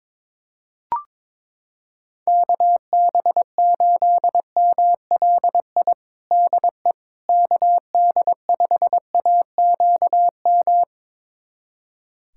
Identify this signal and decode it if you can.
Morse code, heard as a keyed tone, about 22 wpm, 700 Hz. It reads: KB8MLI DE KD5AQM
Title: USA Callsign DE Callsign 22wpm